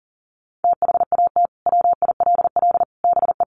Morse code, heard as a keyed tone, about 40 wpm, 700 Hz. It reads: THAT WILL BE
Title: Sentences from Top 1400 Words 40wpm